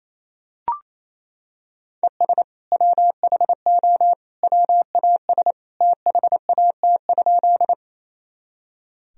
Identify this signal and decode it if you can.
Morse code, heard as a keyed tone, about 28 wpm, 700 Hz. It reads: ES WHO WAS THAT?
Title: Sentences from Top 100 Words and Abbreviations 28wpm